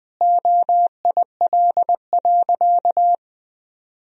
Morse code, heard as a keyed tone, about 20 wpm, 700 Hz. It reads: OIL.
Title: Sentences from Top 1000 Words - Review Part 4 - 20wpm